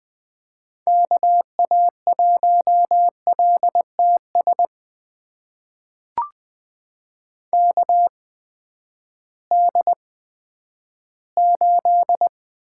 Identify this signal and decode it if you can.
Morse code, heard as a keyed tone, about 20 wpm, spaced as for 11 wpm, 700 Hz. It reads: KA1LTS KD8
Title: USA Call Signs - 6 Characters - 8x Character Spacing - 20wpm